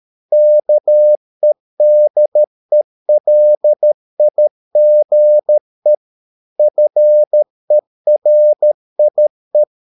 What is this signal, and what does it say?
Morse code, heard as a keyed tone, about 13 wpm, 600 Hz. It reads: KEDELIGE FERIE
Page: Ferie 13 wpm